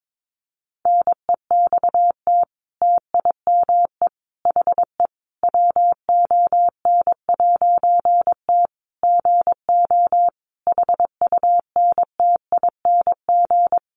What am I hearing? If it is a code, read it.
Morse code, heard as a keyed tone, about 22 wpm, 700 Hz. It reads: NEXT TIME HE WON'T GO HUNTING
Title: A Sound Of Thunder 22wpm